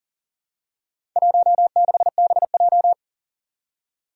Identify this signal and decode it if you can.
Morse code, heard as a keyed tone, about 42 wpm, 700 Hz. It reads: 16BJ
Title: Four Challenging Letter-Number Combinations 40wpm